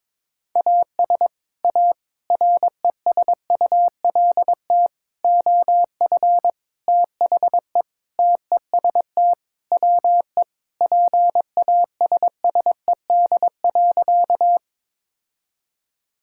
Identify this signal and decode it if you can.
Morse code, heard as a keyed tone, about 22 wpm, 700 Hz. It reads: AS A RESULT OF THE TEST WE PASSED.
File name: practice / 22wpm / Sentences with 4-Word Phrases - 22wpm